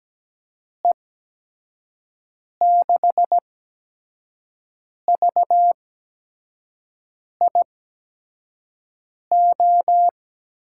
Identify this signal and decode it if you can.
Morse code, heard as a keyed tone, about 17 wpm, spaced as for 5 wpm, 700 Hz. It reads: E6VIO
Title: USA Call Signs - 6 Characters - 8x Character Spacing - 17wpm